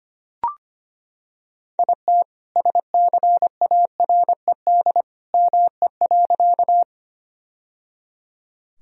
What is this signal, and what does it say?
Morse code, heard as a keyed tone, about 25 wpm, 700 Hz. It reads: IT SCARED ME.
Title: Sentences from Top 1000 Words - Review Part 4 - 25wpm